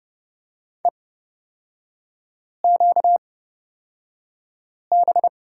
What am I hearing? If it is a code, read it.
Morse code, heard as a keyed tone, about 30 wpm, spaced as for 5 wpm, 700 Hz. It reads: EQB
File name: Lesson 38e - Letter q - Three Characters 30wpm